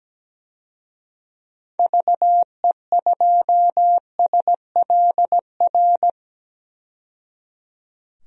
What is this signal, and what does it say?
Morse code, heard as a keyed tone, about 17 wpm, 700 Hz. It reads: VE2SLR
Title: DXCC-100 Call Signs - 6 Characters - 17wpm